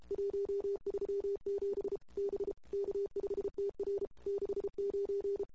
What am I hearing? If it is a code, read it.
Morse code, heard as a keyed tone, about 33 wpm, 380 Hz. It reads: 137 B K5TR 69
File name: CW / 2008-ss-cw-k5tr-op-n5rz -22:00